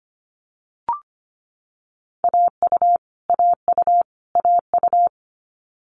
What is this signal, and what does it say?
Morse code, heard as a keyed tone, about 25 wpm, 700 Hz. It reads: AU AU AU
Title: Multi-letter Phonemes Repeated 3x 25wpm